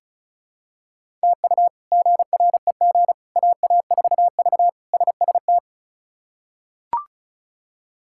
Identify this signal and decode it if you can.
Morse code, heard as a keyed tone, about 36 wpm, 700 Hz. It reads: TU GREG AA4V SST
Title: K1USN SST QSO Practice - 35wpm